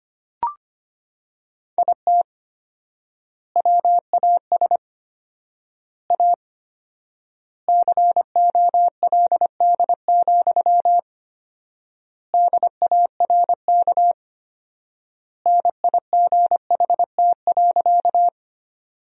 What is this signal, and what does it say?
Morse code, heard as a keyed tone, about 25 wpm, 700 Hz. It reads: IT WAS A COLD, DARK NIGHT.
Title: 4x Word-Spacing - Sentences from Top 500 Words - 25wpm